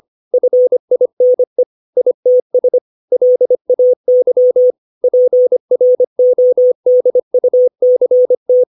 Morse code, heard as a keyed tone, about 25 wpm, 500 Hz. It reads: FINE ITS LAY PRODUCT
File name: 632 words / 632 Words 25WPM